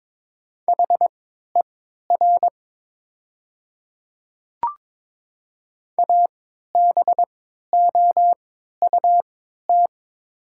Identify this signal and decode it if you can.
Morse code, heard as a keyed tone, about 22 wpm, spaced as for 13 wpm, 700 Hz. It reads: HER ABOUT
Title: 3x Character Spacing - Top 100 Words - 22wpm